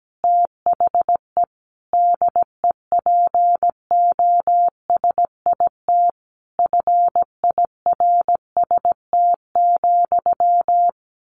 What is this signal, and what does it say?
Morse code, heard as a keyed tone, about 17 wpm, 700 Hz. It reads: THE DEPOSIT FIRST,
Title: Sentences from Top 1300 Words 17wpm